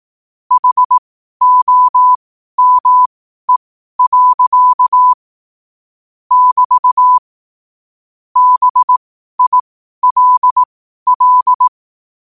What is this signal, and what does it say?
Morse code, heard as a keyed tone, about 18 wpm, spaced as for 12 wpm, 1000 Hz. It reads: HOME. <BT> BILL
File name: 2024-09-14-12wpm